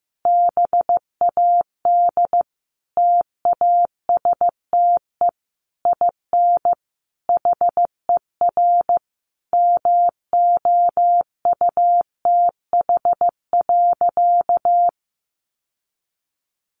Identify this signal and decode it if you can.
Morse code, heard as a keyed tone, about 15 wpm, 700 Hz. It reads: BAD TASTE IN HER MOUTH.
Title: Sentences from Top 1500 Words 15wpm